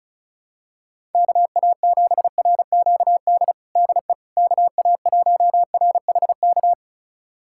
Morse code, heard as a keyed tone, about 36 wpm, 700 Hz. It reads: KA7RQD DE KA1RHK
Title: USA Callsign DE Callsign - 6 Characters - 35wpm